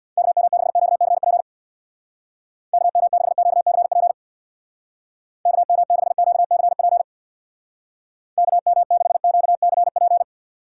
Morse code, heard as a keyed tone, about 63 wpm, 700 Hz. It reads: KM6QCP KM6QCP KM6QCP KM6QCP